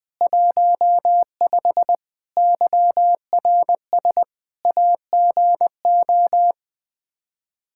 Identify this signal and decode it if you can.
Morse code, heard as a keyed tone, about 20 wpm, 700 Hz. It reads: 15 YRS AGO
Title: K1USN SST QSO Practice 20wpm